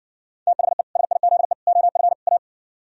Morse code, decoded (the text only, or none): THE SIZE OF A